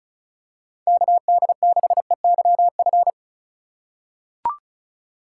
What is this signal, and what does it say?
Morse code, heard as a keyed tone, about 34 wpm, 700 Hz. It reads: KD6EYF
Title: USA Call Signs - 6 Characters - 2x Character Spacing - 35wpm